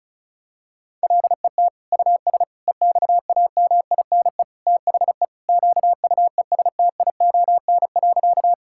Morse code, heard as a keyed tone, about 36 wpm, 700 Hz. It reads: LET US EXAMINE THE QUESTION.
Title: Sentences from Top 1600 Words 35wpm